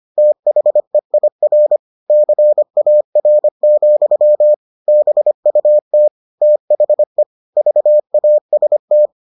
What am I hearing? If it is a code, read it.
Morse code, heard as a keyed tone, about 25 wpm, 600 Hz. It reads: THEIR CAR, BUT THE VAST